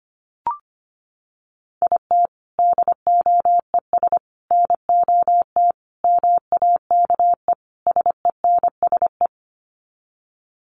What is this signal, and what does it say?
Morse code, heard as a keyed tone, about 25 wpm, 700 Hz. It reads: IT DOES NOT MAKE SENSE
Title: Sets of 5 Words Encore2 25wpm